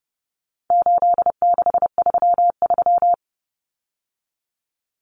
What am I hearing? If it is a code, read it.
Morse code, heard as a keyed tone, about 30 wpm, 700 Hz. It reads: 8633
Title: Numbers - 4-Digit 30wpm